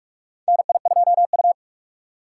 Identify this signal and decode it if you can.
Morse code, heard as a keyed tone, about 47 wpm, 700 Hz. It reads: NI2U
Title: USA Call Signs - 4 Characters - 2x Character Spacing - 45wpm